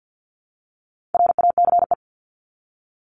Morse code, heard as a keyed tone, about 50 wpm, 700 Hz. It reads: RACE
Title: Top 1501-1750 Words 50wpm